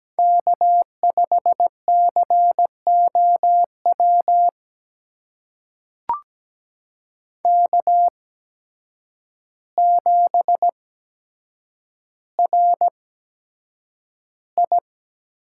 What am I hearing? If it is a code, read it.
Morse code, heard as a keyed tone, about 17 wpm, spaced as for 8 wpm, 700 Hz. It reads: K5COW K7RI
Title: USA Call Signs - 5 Characters - 8x Character Spacing - 17wpm